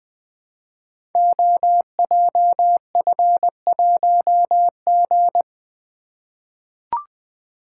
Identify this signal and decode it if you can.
Morse code, heard as a keyed tone, about 20 wpm, 700 Hz. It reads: OJF1G
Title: Code Groups - 5 Letter:Numbers - 20wpm